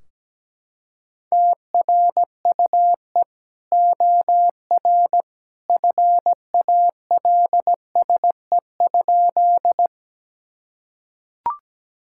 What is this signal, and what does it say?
Morse code, heard as a keyed tone, about 17 wpm, 700 Hz. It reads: TRUE OR FALSE?
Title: Sentences from Top 1600 Words 17wpm